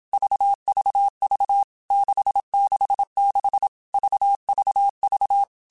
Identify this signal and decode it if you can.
Morse code, heard as a keyed tone, about 27 wpm, 800 Hz. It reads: VVV 666 VVV
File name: Basic Letter Groups By7Groups / VB46 / VB46@25